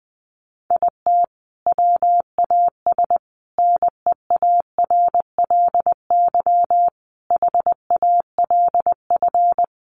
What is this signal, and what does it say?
Morse code, heard as a keyed tone, about 20 wpm, 700 Hz. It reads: IT WAS NEARLY HALF
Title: Sentences from Top 1300 Words 20wpm